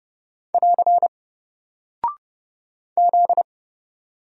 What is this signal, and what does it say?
Morse code, heard as a keyed tone, about 30 wpm, 700 Hz. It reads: <AR> Z
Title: Single Character (Letter:Number:Punctuation:Prosign) 30wpm